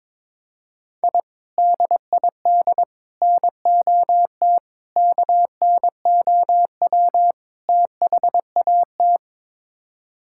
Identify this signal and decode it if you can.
Morse code, heard as a keyed tone, about 22 wpm, 700 Hz. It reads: I DID NOT KNOW THAT
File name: practice / 22wpm / Build A Phrase - 5-Word Phrases - 22wpm